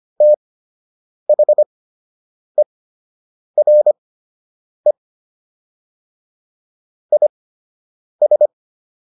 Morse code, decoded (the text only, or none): THERE IS